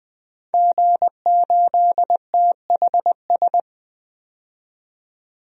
Morse code, heard as a keyed tone, about 20 wpm, 700 Hz. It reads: G8THS
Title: DXCC-100 Call Signs - 5 Characters - 20wpm